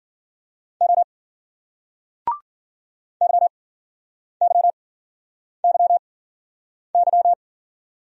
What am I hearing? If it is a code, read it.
Morse code, heard as a keyed tone, about 54 wpm, spaced as for 9 wpm, 700 Hz. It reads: O YYYY